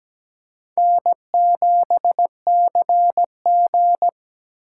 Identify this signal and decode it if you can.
Morse code, heard as a keyed tone, about 17 wpm, 700 Hz. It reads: N7CG
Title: USA Call Signs - 4 Characters - 8x Character Spacing - 17wpm